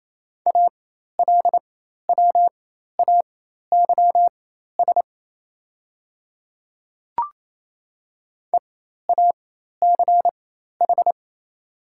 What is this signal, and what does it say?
Morse code, heard as a keyed tone, about 27 wpm, spaced as for 13 wpm, 700 Hz. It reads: ALWAYS EACH
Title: 4x Character Spacing - Top 300 Words - 28wpm